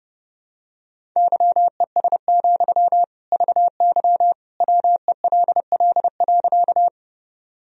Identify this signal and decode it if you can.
Morse code, heard as a keyed tone, about 30 wpm, 700 Hz. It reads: YES, VY WELL.
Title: Sentences from Top 500 Words and Abbreviations 30wpm